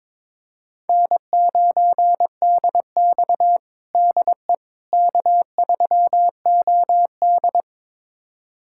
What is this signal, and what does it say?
Morse code, heard as a keyed tone, about 22 wpm, 700 Hz. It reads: N9DX DE K3OD
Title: USA Callsign DE Callsign - 4 Characters - 22wpm